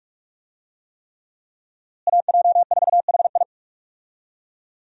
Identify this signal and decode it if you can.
Morse code, heard as a keyed tone, about 46 wpm, 700 Hz. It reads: AJ4HI
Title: USA Call Signs - 5 Characters - Repeated 3x - 45wpm